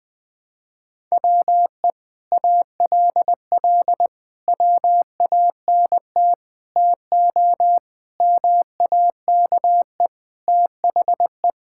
Morse code, heard as a keyed tone, about 20 wpm, 700 Hz. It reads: WE ALL WANT TO MAKE THE
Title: Sentences with 7-Word Phrases - Spoken Beginning - 20wpm